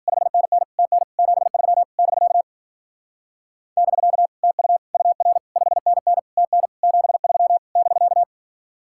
Morse code, heard as a keyed tone, about 50 wpm, 700 Hz. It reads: 5NN TN 73 <BK> <BK> TU UR 5NN TN 73 <BK>
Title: Speed-Racing Common QSO Phrases - 40wpm